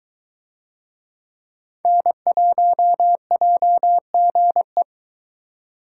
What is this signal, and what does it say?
Morse code, heard as a keyed tone, about 22 wpm, 700 Hz. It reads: N1JGE